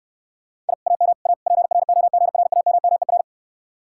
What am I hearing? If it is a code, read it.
Morse code, heard as a keyed tone, about 68 wpm, 700 Hz. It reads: I AM A PROGRAMMER